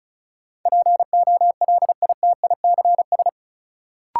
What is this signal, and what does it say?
Morse code, heard as a keyed tone, about 36 wpm, 700 Hz. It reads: POLITICS
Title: Sentences from Top 1000 Words - New Word Drill - 35wpm